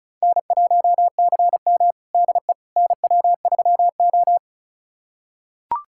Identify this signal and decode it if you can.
Morse code, heard as a keyed tone, about 36 wpm, 700 Hz. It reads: N1CM DE NW3O
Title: USA Callsign DE Callsign - 4 Characters - 35wpm